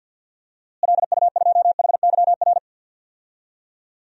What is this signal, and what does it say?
Morse code, heard as a keyed tone, about 48 wpm, 700 Hz. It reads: LU2HYR
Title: DXCC-100 Call Signs - 6 Characters - Repeated 3x - 50wpm